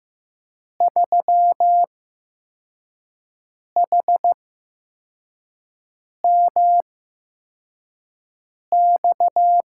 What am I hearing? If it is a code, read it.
Morse code, heard as a keyed tone, about 15 wpm, spaced as for 4 wpm, 700 Hz. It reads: 3HMX
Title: USA Call Signs - 8x Character Spacing - 15wpm